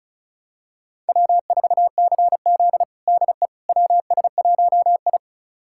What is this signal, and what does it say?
Morse code, heard as a keyed tone, about 34 wpm, 700 Hz. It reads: W4CZ DE WS1I
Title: USA Callsign DE Callsign - 4 Characters - 35wpm